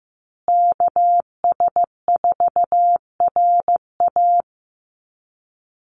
Morse code, heard as a keyed tone, about 15 wpm, 700 Hz. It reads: KS4RA
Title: USA Call Signs Repeated 2x 15wpm